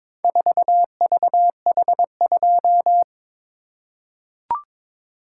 Morse code, heard as a keyed tone, about 22 wpm, 700 Hz. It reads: 4VH2
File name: practice / 22wpm / Four Letter-Number Combinations - 22wpm